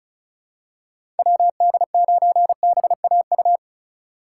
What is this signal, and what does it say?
Morse code, heard as a keyed tone, about 36 wpm, 700 Hz. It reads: WD9BAU
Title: USA Call Signs - 6 Characters - 4x Character Spacing - 35wpm